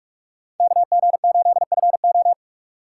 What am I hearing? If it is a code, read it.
Morse code, heard as a keyed tone, about 46 wpm, 700 Hz. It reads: KG8FO
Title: USA Call Signs - 5 Characters - 2x Character Spacing - 45wpm